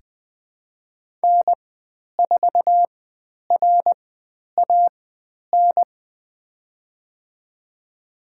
Morse code, heard as a keyed tone, about 20 wpm, spaced as for 10 wpm, 700 Hz. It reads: N4RAN
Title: 095 Lesson 16g - Letter c - Call Signs 20wpm